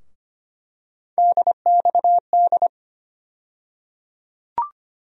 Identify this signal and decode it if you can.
Morse code, heard as a keyed tone, about 25 wpm, 700 Hz. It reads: DXD
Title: USA Call Sign Postfixes - 3 Letters 25wpm